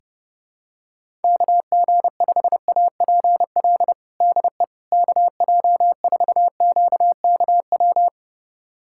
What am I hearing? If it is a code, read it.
Morse code, heard as a keyed tone, about 32 wpm, 700 Hz. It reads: KG5APL DE KJ4QKW
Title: USA Callsign DE Callsign 30wpm